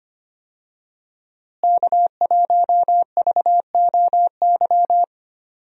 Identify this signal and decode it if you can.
Morse code, heard as a keyed tone, about 25 wpm, 700 Hz. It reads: K1VOY
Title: USA Call Signs - 5 Characters - Repeated 2x - 25wpm